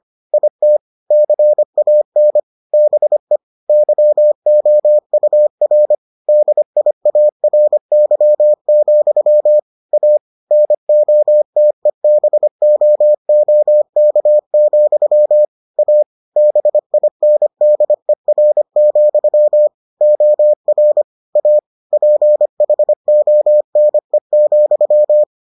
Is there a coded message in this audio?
Yes: Morse code from 0.3 to 25.2 s, heard as a keyed tone, about 25 wpm, 600 Hz. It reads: IT CAN BE YOUR DIARY, A NOTEBOOK, A BINDER, OR A PHONE,